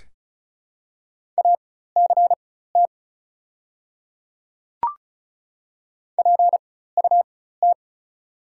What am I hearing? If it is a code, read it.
Morse code, heard as a keyed tone, about 36 wpm, spaced as for 16 wpm, 690 Hz. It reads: ACT PUT
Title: 4x Character Spacing - Top 500 Words - 35wpm